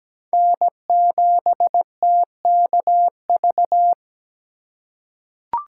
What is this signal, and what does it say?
Morse code, heard as a keyed tone, about 17 wpm, 700 Hz. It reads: N7TKV
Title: USA Call Signs - 5 Characters - 4x Character Spacing - 17wpm